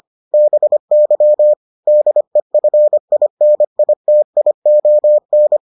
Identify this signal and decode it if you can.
Morse code, heard as a keyed tone, about 25 wpm, 600 Hz. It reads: BY DEFINITION